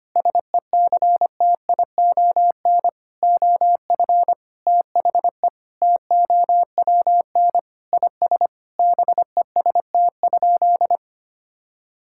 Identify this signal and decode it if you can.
Morse code, heard as a keyed tone, about 25 wpm, 700 Hz. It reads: SECTION OF THE TOWN IS BEST?
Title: Sentences from Top 1000 Words - Review Part 5 - 25wpm